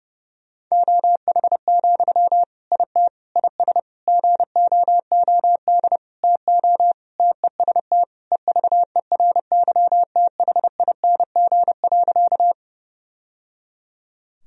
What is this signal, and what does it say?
Morse code, heard as a keyed tone, about 30 wpm, 700 Hz. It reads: OH, IT IS GOOD TO TEST EVERYTHING.
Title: Sentences from Top 1000 Words - Review Part 3 - 30wpm